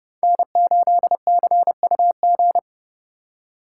Morse code, heard as a keyed tone, about 30 wpm, 700 Hz. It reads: N8CUG
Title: USA Call Signs - 5 Characters - 4x Character Spacing - 30wpm